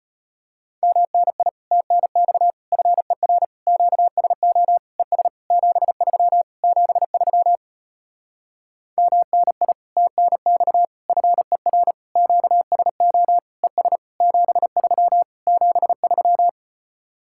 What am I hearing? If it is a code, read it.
Morse code, heard as a keyed tone, about 33 wpm, 700 Hz. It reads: MNI TNX FER QSO ES 73 73 MNI TNX FER QSO ES 73 73